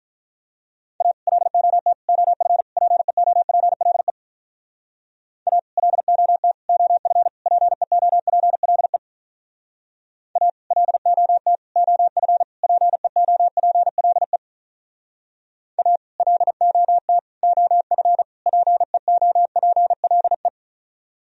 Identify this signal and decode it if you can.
Morse code, heard as a keyed tone, about 43 wpm, 700 Hz. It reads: A LOT OF PEOPLE A LOT OF PEOPLE A LOT OF PEOPLE A LOT OF PEOPLE
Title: Speed-Racing - Sets of 4 Words 35wpm